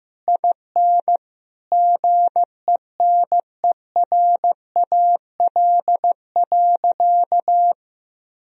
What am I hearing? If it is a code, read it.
Morse code, heard as a keyed tone, about 15 wpm, 700 Hz. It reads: IN GENERAL.
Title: Sentences from Top 1000 Words - Review Part 3 - 15wpm